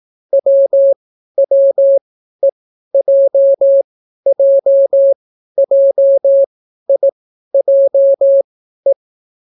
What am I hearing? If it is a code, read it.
Morse code, heard as a keyed tone, about 18 wpm, spaced as for 12 wpm, 550 Hz. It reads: WWEJJJIJE